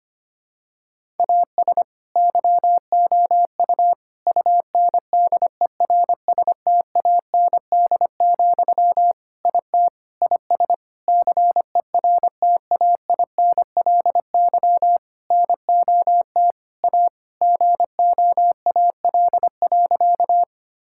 Morse code, heard as a keyed tone, about 25 wpm, 700 Hz. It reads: AS YOU UNDERSTAND, IT IS CERTAINLY NOT A GOAL.
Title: Sentences from Top 1000 Words - Review Part 5 - 25wpm